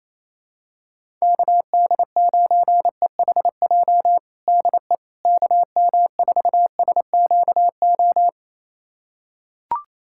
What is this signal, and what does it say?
Morse code, heard as a keyed tone, about 28 wpm, 700 Hz. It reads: KD9EHJ DE KM4SQO
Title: USA Callsign DE Callsign 28wpm